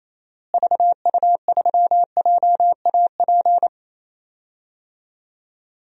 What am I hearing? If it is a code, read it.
Morse code, heard as a keyed tone, about 28 wpm, 700 Hz. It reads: VU3JAP
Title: DXCC-100 Call Signs - 6 Characters - Repeated 3x - 28wpm